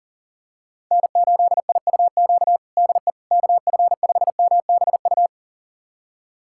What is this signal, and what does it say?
Morse code, heard as a keyed tone, about 37 wpm, 700 Hz. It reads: N8IUQ DE KF5MBU
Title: USA Callsign DE Callsign 40wpm